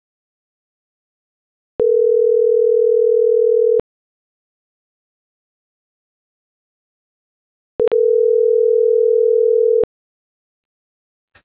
Ringback tone (North America), 440 + 480 Hz.